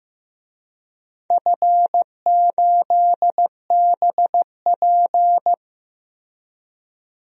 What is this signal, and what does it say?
Morse code, heard as a keyed tone, about 15 wpm, 700 Hz. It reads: F8BP
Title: DXCC-100 Call Signs - 4 Characters - 15wpm